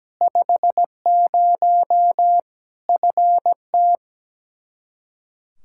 Morse code, heard as a keyed tone, about 17 wpm, 700 Hz. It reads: 50 FT
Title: Practice QSO - Exchange 2 - Info - 17wpm